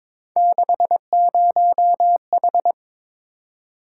Morse code, heard as a keyed tone, about 26 wpm, 700 Hz. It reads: 60H
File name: Speed-Racing - Three Letter-Number Combinations - 22wpm